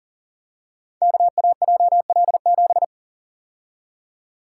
Morse code, heard as a keyed tone, about 42 wpm, 700 Hz. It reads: KAJL7
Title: Code Groups - 5 Letter:Numbers - 40wpm